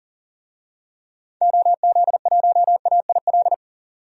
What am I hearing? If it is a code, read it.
Morse code, heard as a keyed tone, about 40 wpm, 700 Hz. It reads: OZ1AIL
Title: DXCC-100 Call Signs - 6 Characters - 40wpm